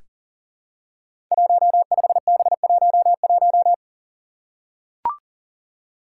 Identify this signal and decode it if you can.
Morse code, heard as a keyed tone, about 40 wpm, 700 Hz. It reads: 15B11
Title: Five Challenging Letter-Number Combinations 40wpm